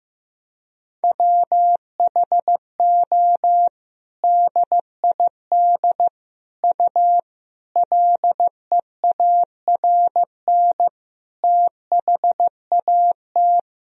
Morse code, heard as a keyed tone, about 15 wpm, 700 Hz. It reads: WHO DID U LEARN THAT
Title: Sentences from Top 300 Words and Abbreviations 15wpm